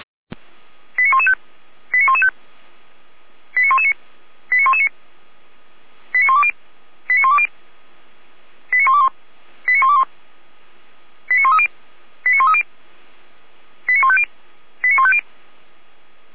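Five-tone selective call (ZVEI), heard as quick runs of five tones, tones of about 70 ms. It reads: 89115 89115 89119 89119 89120 89120 89121 89121 89130 89130 89150 89150
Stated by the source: ZVEI five-tone selective call received by SDR, from audio.f85360.feuerwehr-orlamuende.org.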